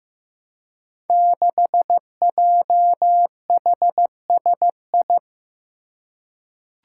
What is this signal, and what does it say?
Morse code, heard as a keyed tone, about 15 wpm, 700 Hz. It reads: 6JHSI
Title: Five Challenging Letter-Number Combinations 15wpm